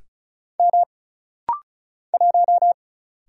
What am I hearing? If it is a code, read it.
Morse code, heard as a keyed tone, about 35 wpm, 700 Hz. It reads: M 1
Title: Single Letter-Number - Rapid-Fire 35wpm